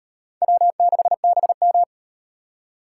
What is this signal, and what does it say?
Morse code, heard as a keyed tone, about 36 wpm, 700 Hz. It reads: W6BM